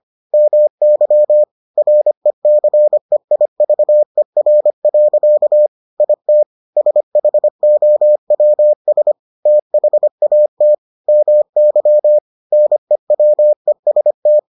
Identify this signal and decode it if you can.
Morse code, heard as a keyed tone, about 25 wpm, 600 Hz. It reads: MY RECEIVER. IT SHOWS THAT MY NEWEST